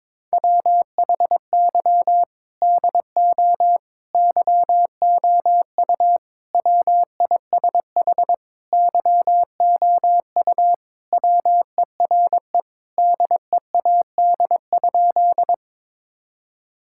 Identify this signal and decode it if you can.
Morse code, heard as a keyed tone, about 21 wpm, 700 Hz. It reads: WHY DO YOU WISH YOU WERE DEAD?
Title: Sentences from Top 1600 Words 22wpm